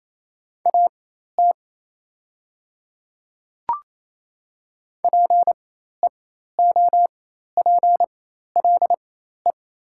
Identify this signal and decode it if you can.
Morse code, heard as a keyed tone, about 28 wpm, spaced as for 13 wpm, 690 Hz. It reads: AT PEOPLE
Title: 4x Character Spacing - Top 100 Words - 28wpm